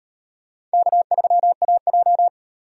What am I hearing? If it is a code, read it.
Morse code, heard as a keyed tone, about 33 wpm, 700 Hz. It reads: K3AJ